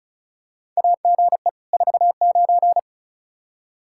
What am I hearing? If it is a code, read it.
Morse code, heard as a keyed tone, about 36 wpm, 700 Hz. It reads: AGE 49
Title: Practice QSO - Exchange 3 - More Info - 35wpm